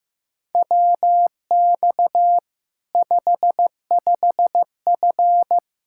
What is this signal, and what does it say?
Morse code, heard as a keyed tone, about 15 wpm, 700 Hz. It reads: WX 55F
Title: Practice QSO - Exchange 2 - Info - 15wpm